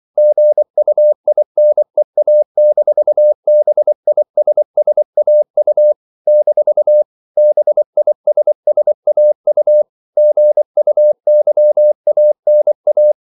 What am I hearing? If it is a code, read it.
Morse code, heard as a keyed tone, about 24 wpm, 600 Hz. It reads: GUINEA-BISSAU - BISSAU GUYANA
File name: CW / Practice / NationalCapitals24